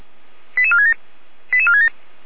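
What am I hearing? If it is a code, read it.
Five-tone selective call (ZVEI), heard as quick runs of five tones, tones of about 70 ms. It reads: 88467 88467
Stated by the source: ZVEI five-tone selective call received on an SDR.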